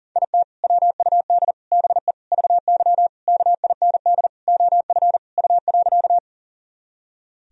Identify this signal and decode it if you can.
Morse code, heard as a keyed tone, about 40 wpm, 700 Hz. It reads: IT WUD BE VY KIND OF U.
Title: Sentences from Top 200 Words and Abbreviations 40wpm